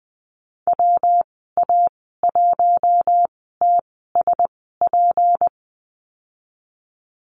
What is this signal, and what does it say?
Morse code, heard as a keyed tone, about 20 wpm, spaced as for 15 wpm, 700 Hz. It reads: WA1TSP
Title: 161 Lesson 26g - Letter g - Call Signs 20wpm